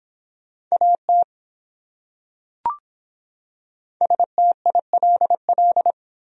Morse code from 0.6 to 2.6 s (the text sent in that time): AT